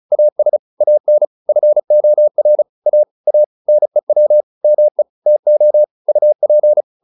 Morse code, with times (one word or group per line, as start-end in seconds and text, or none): 0.1-0.6 AS
0.8-1.2 AN
1.5-2.6 FOR
2.9-3.0 A
3.3-3.4 A
3.7-4.4 NEW
4.6-5.0 ME
5.3-5.8 TO
6.1-6.8 UP